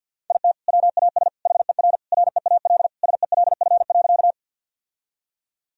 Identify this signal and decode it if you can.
Morse code, heard as a keyed tone, about 51 wpm, 700 Hz. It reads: IT WAS HER REAL SELF.
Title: Sentences from Top 1000 Words - Review Part 1 - 50wpm